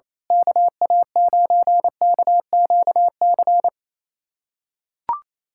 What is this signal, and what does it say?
Morse code, heard as a keyed tone, about 28 wpm, 700 Hz. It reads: KA9KQC
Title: USA Callsign DE Callsign - 6 Characters - 28wpm